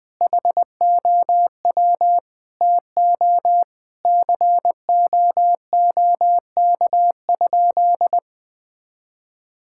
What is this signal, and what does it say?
Morse code, heard as a keyed tone, about 20 wpm, 700 Hz. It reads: HOW TO COOK?
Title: Sentences with 6-Word Phrases - 20wpm